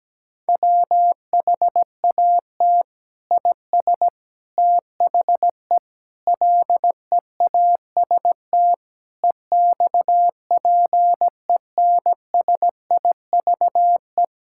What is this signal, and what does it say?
Morse code, heard as a keyed tone, about 17 wpm, 700 Hz. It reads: WHAT IS THE LEAST EXPENSIVE
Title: Sentences from Top 1000 Words - Review Part 6 - 17wpm